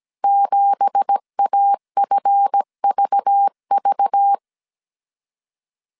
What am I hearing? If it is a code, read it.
Morse code, heard as a keyed tone, about 17 wpm, 800 Hz. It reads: 7AFVV
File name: CW / DET / 1 9 1500C S A 16wpm 2828